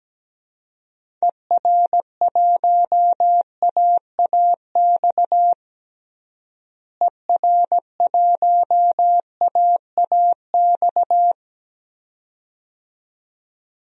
Morse code, heard as a keyed tone, about 17 wpm, 700 Hz. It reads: ER1AAX ER1AAX
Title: DXCC-100 Call Signs - 6 Characters - Repeated 2x - 17wpm